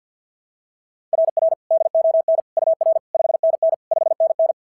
Morse code, heard as a keyed tone, about 47 wpm, 690 Hz. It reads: RR DON UR 5NN 5NN
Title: SOTA QSO Practice 50wpm